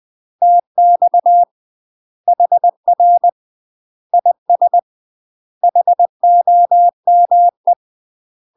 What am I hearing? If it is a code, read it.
Morse code, heard as a keyed tone, about 20 wpm, 700 Hz. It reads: TX HR IS HOME
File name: UK MASTER QSOs TRAINING / 20WPM / UK MASTER QSOs 0011